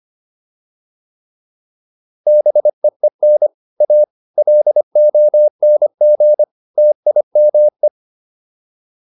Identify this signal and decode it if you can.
Morse code, heard as a keyed tone, about 25 wpm, 600 Hz. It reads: BEEN A LONG TIME